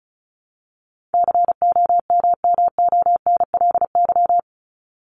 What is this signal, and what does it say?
Morse code, heard as a keyed tone, about 34 wpm, 700 Hz. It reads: COMMONLY
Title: LY Suffix - Top 100 Words with Base Words 35wpm